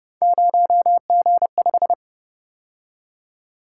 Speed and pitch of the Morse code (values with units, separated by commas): 31 wpm, 700 Hz